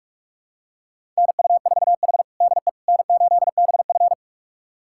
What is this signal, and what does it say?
Morse code, heard as a keyed tone, about 41 wpm, 700 Hz. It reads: NU4H DE N8BF
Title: USA Callsign DE Callsign - 4 Characters - 45wpm